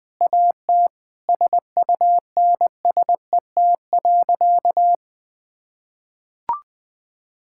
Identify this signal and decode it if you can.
Morse code, heard as a keyed tone, about 20 wpm, 700 Hz. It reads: AT SUNSET.
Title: Sentences with 6-Word Phrases - 20wpm